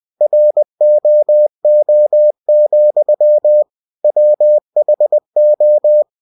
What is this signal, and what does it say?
Morse code, heard as a keyed tone, about 20 wpm, 600 Hz. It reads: ROO, WHO
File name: cw-books / winnie-the-pooh / winnie-20wpm-page-0107